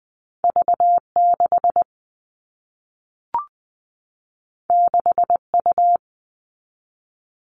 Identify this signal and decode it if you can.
Morse code, heard as a keyed tone, about 20 wpm, 700 Hz. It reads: V6 6U
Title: All Cut Numbers - 2-Digit 20wpm